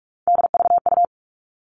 Morse code, heard as a keyed tone, about 44 wpm, 700 Hz. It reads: DVU
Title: All Cut Numbers - 3-Digit 45wpm